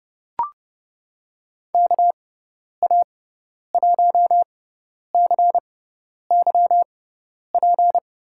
Morse code, heard as a keyed tone, about 30 wpm, spaced as for 10 wpm, 700 Hz. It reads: KA1CYP
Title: USA Call Signs - 6 Characters - 6x Character Spacing - 30wpm